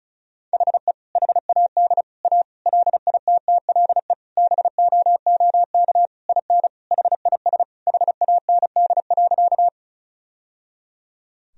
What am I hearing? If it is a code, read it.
Morse code, heard as a keyed tone, about 36 wpm, 700 Hz. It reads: HE HAD A LITTLE BOOK IN HIS HAND.
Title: Sentences from Top 1000 Words - Review Part 1 - 35wpm